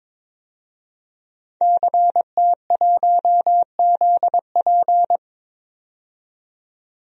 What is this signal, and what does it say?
Morse code, heard as a keyed tone, about 22 wpm, 700 Hz. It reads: CT1ZP
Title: DXCC-100 Call Signs - 5 Characters - 22wpm